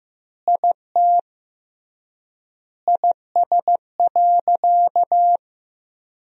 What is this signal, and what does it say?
Morse code, heard as a keyed tone, about 15 wpm, 700 Hz. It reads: IT IS.
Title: 3x Word-Spacing - Sentences from Top 400 Words - 15wpm